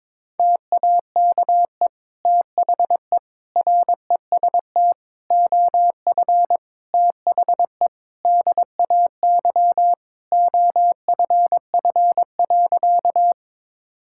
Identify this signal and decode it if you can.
Morse code, heard as a keyed tone, about 22 wpm, 700 Hz. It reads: TAKE THE REST OF THE DAY OFF.
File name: practice / 22wpm / Sentences with 2-Word Phrases - 22wpm